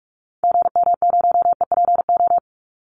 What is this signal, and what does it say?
Morse code, heard as a keyed tone, about 44 wpm, 700 Hz. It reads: GM0EPO
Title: Call Signs - 45wpm